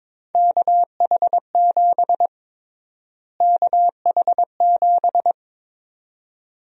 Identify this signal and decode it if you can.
Morse code, heard as a keyed tone, about 22 wpm, 700 Hz. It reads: KH7 KH7
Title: USA Call Sign Prefixes - Repeated 2x - 22wpm